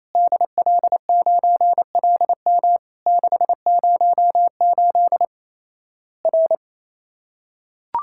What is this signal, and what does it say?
Morse code, heard as a keyed tone, about 28 wpm, 700 Hz. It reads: DL9LM 608 R
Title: Worked All Europe DX Contest - QTC Exchange - 28wpm